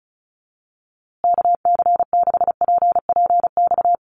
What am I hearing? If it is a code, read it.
Morse code, heard as a keyed tone, about 34 wpm, 700 Hz. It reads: KC6PPX
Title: USA Call Signs Repeated 3x 35wpm